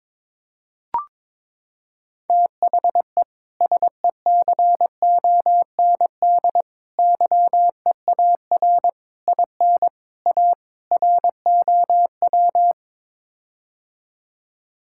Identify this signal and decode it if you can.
Morse code, heard as a keyed tone, about 22 wpm, 700 Hz. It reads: THE SECOND YEAR IN A ROW
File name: practice / 22wpm / Sets of 6 Words - 22wpm